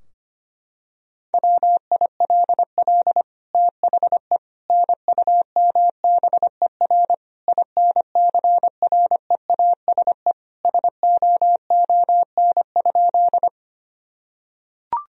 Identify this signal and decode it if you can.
Morse code, heard as a keyed tone, about 25 wpm, 700 Hz. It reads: WILL THE NUMBER INCREASE SOON?
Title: Sentences from Top 1000 Words - Review Part 4 - 25wpm